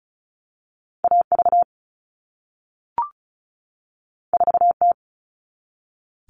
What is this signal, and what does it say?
Morse code, heard as a keyed tone, about 36 wpm, 700 Hz. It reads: AV 4T
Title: All Cut Numbers - 2-Digit 35wpm